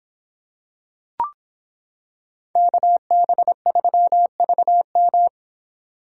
Morse code, heard as a keyed tone, about 26 wpm, 700 Hz. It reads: KB3VM